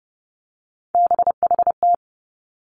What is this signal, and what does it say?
Morse code, heard as a keyed tone, about 30 wpm, 700 Hz. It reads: BHT
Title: Three Challenging Letter-Number Combinations 30wpm